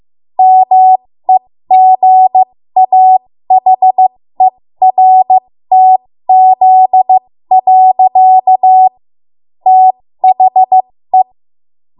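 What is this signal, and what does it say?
Morse code, heard as a keyed tone, about 15 wpm, spaced as for 13 wpm, 750 Hz. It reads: MEGAHERTZ. THE
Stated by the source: 13 wpm Morse code practice archive transmission.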